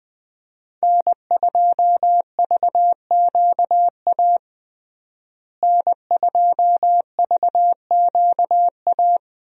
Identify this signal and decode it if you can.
Morse code, heard as a keyed tone, about 21 wpm, 700 Hz. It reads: N2VQA N2VQA
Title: USA Call Signs - 5 Characters - Repeated 2x - 20wpm